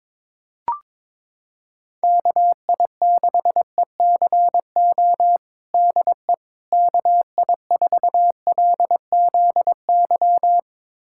Morse code, heard as a keyed tone, about 22 wpm, 700 Hz. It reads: KI6ECO DE KI4LZY
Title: USA Callsign DE Callsign - 6 Characters - 22wpm